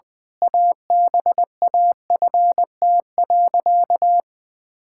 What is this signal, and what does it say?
Morse code, heard as a keyed tone, about 21 wpm, 700 Hz. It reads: ABAFT.
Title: Pangram Challenge v2 - 20wpm